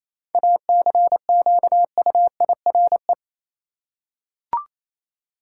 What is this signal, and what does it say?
Morse code, heard as a keyed tone, about 28 wpm, 700 Hz. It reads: ACQUIRE
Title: Seven Letter Words - 28wpm